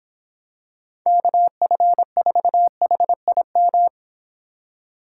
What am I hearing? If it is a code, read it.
Morse code, heard as a keyed tone, about 25 wpm, 700 Hz. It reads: KF4HIM